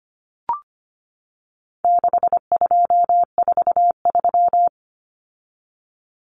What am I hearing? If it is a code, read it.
Morse code, heard as a keyed tone, about 25 wpm, 700 Hz. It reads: 6243
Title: Numbers - 4-Digit 25wpm